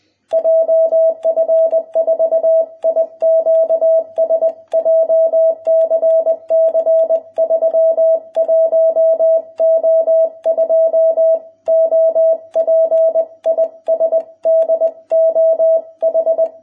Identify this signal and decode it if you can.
Morse code, heard as a keyed tone, about 20 wpm, 660 Hz. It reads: JF4IQS JCC 31O2 OP IS DOH